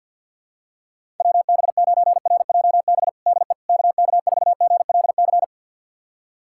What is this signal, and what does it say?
Morse code, heard as a keyed tone, about 50 wpm, 700 Hz. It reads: WB9RJB DE KK4GLC
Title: USA Callsign DE Callsign 50wpm